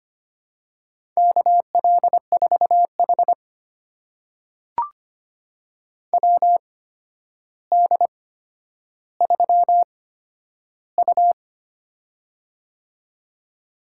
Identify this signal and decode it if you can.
Morse code, heard as a keyed tone, about 25 wpm, spaced as for 11 wpm, 700 Hz. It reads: KL4H WD3U
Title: USA Call Signs - 4 Characters - 8x Character Spacing - 25wpm